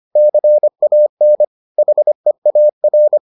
Morse code, heard as a keyed tone, about 25 wpm, 600 Hz. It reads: CAN HEAR